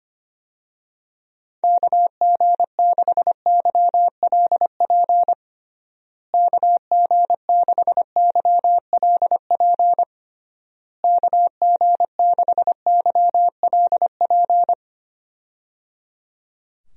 Morse code, heard as a keyed tone, about 25 wpm, 700 Hz. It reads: KG6YLP KG6YLP KG6YLP
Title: USA Call Signs - 6 Characters - Repeated 3x - 25wpm